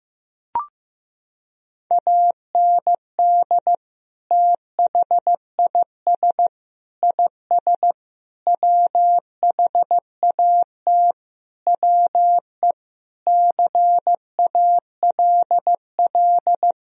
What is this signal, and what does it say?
Morse code, heard as a keyed tone, about 16 wpm, 700 Hz. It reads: AND THIS IS WHAT WE CALL
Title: Sentences from Top 1000 Words - Review Part 3 - 15wpm